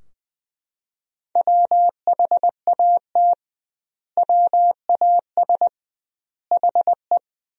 Morse code, heard as a keyed tone, about 20 wpm, 700 Hz. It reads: WHAT WAS HE
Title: 2x Word-Spacing - Sentences from Top 100 Words - 20wpm